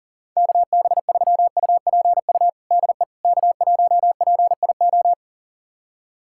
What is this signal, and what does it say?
Morse code, heard as a keyed tone, about 40 wpm, 700 Hz. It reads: KB3UPU DE K1PIO
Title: USA Callsign DE Callsign 40wpm